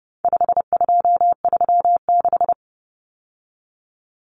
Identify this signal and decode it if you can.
Morse code, heard as a keyed tone, about 30 wpm, 700 Hz. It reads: 5236
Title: Numbers - 4-Digit 30wpm